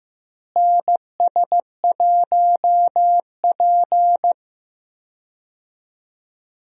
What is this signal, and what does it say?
Morse code, heard as a keyed tone, about 15 wpm, 700 Hz. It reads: NS1P
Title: USA Call Signs - 4 Characters - Repeated 3x - 15wpm